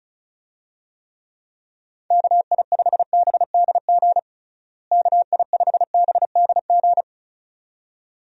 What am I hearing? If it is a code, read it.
Morse code, heard as a keyed tone, about 36 wpm, 700 Hz. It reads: KI5BDG KI5BDG
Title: USA Call Signs - 6 Characters - Repeated 2x - 35wpm